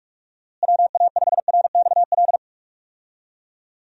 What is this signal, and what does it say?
Morse code, heard as a keyed tone, about 46 wpm, 700 Hz. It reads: WA5RXL
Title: USA Call Signs - 6 Characters - Repeated 3x - 45wpm